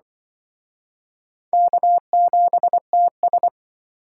Morse code, heard as a keyed tone, about 24 wpm, 700 Hz. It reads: K7TS
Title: Speed Racing - USA 1x2 Call Signs 20wpm